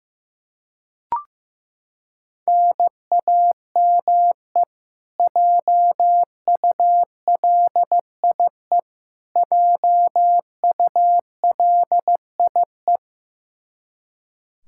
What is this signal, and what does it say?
Morse code, heard as a keyed tone, about 15 wpm, 700 Hz. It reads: NAME JULIE JULIE
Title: Practice QSO - Exchange 1 - Beginning - 15wpm